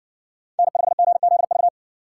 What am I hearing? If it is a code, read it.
Morse code, heard as a keyed tone, about 60 wpm, 700 Hz. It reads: N5GZV